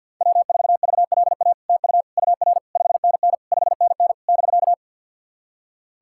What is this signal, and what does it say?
Morse code, heard as a keyed tone, about 50 wpm, 700 Hz. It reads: W4VLA TU UR 5NN 5NN <BK>
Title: SOTA QSO Practice 50wpm